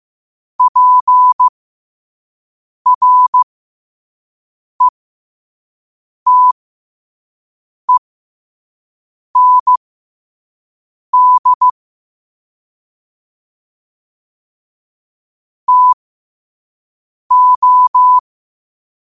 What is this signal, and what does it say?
Morse code, heard as a keyed tone, about 15 wpm, spaced as for 5 wpm, 1000 Hz. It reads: PRETEND TO
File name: qotdfiles / 5wpm